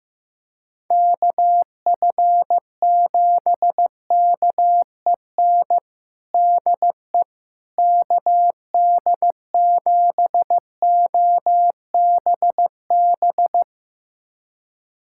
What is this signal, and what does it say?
Morse code, heard as a keyed tone, about 16 wpm, 700 Hz. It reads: KF7KEN DE KD7OBB
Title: USA Callsign DE Callsign - 6 Characters - 15wpm